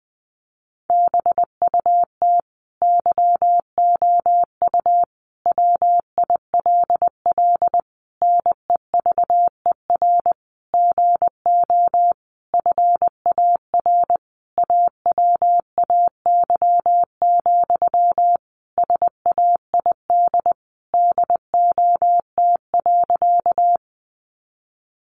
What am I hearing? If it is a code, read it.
Morse code, heard as a keyed tone, about 20 wpm, 700 Hz. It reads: BUT YOU WILL NEVER GO FAR AWAY, SAID DOT.
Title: Sentences from Top 1500 Words 20wpm